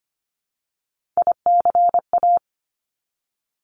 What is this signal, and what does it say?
Morse code, heard as a keyed tone, about 25 wpm, 700 Hz. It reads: ICA
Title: Top Three Letter Combinations in Words 25wpm